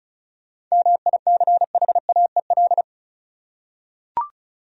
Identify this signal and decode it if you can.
Morse code, heard as a keyed tone, about 36 wpm, 700 Hz. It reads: MICHAEL
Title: USA Common Names - 35wpm